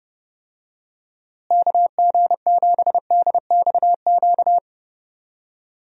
Morse code, heard as a keyed tone, about 32 wpm, 700 Hz. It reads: KG7DXQ
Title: USA Call Signs - 6 Characters - Repeated 2x - 30wpm